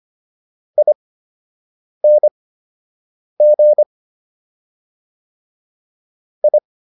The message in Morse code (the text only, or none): ING I